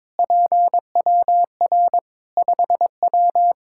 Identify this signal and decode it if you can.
Morse code, heard as a keyed tone, about 22 wpm, 700 Hz. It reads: PWR 5W
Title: K1USN SST QSO Practice 22wpm